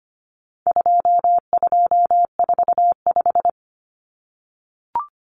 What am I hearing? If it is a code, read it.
Morse code, heard as a keyed tone, about 25 wpm, 700 Hz. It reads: 2245
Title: Numbers - 4-Digit 25wpm